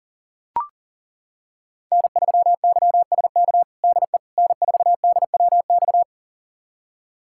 Morse code, heard as a keyed tone, about 40 wpm, 700 Hz. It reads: N3YSK DE N4DWX
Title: USA Callsign DE Callsign - 5 Characters - 40wpm